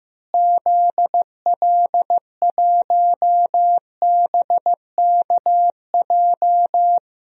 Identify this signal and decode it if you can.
Morse code, heard as a keyed tone, about 15 wpm, 700 Hz. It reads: ZL1BKJ
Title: DXCC-100 Call Signs - 6 Characters - Repeated 2x - 15wpm